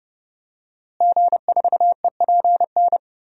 Morse code, heard as a keyed tone, about 30 wpm, 700 Hz. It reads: G4EPN
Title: DXCC-100 Call Signs - 5 Characters - Repeated 2x - 30wpm